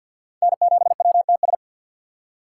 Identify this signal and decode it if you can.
Morse code, heard as a keyed tone, about 47 wpm, 700 Hz. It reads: N7WTS
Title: USA Call Signs - 2x Character Spacing - 50wpm